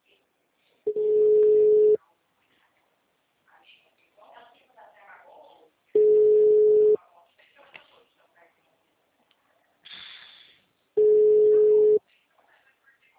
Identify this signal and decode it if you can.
Ringback tone (Europe, 425 Hz), with speech.